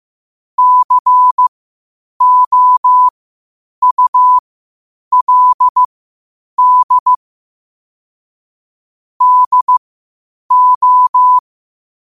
Morse code, heard as a keyed tone, about 15 wpm, spaced as for 8 wpm, 1000 Hz. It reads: COULD DO
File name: qotdfiles / 8wpm